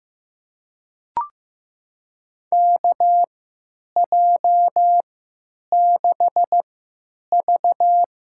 Morse code, heard as a keyed tone, about 15 wpm, spaced as for 9 wpm, 700 Hz. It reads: KJ6V
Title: USA Call Signs - 6 Characters - 3x Character Spacing - 15wpm